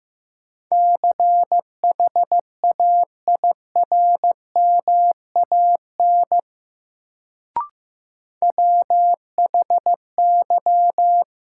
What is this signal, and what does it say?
Morse code, heard as a keyed tone, about 15 wpm, 700 Hz. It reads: CHAIRMAN WHY
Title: Sets of 3 Words - New Word Drill - 15wpm